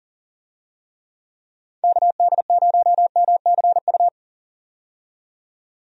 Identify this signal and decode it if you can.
Morse code, heard as a keyed tone, about 40 wpm, 700 Hz. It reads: KD0MCU
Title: USA Call Signs - 6 Characters - 40wpm